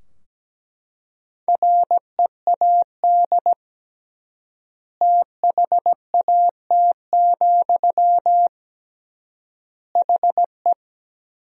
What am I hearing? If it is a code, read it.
Morse code, heard as a keyed tone, about 17 wpm, 700 Hz. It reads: READ THAT, HE
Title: 3x Word-Spacing - Sentences from Top 200 Words - 17wpm